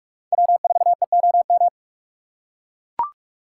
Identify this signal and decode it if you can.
Morse code, heard as a keyed tone, about 41 wpm, 700 Hz. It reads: W4EOM